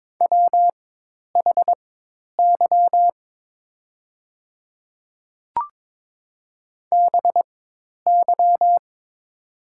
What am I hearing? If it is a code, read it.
Morse code, heard as a keyed tone, about 22 wpm, spaced as for 10 wpm, 690 Hz. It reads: WHY BY
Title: 4x Character Spacing - Top 500 Words - 22wpm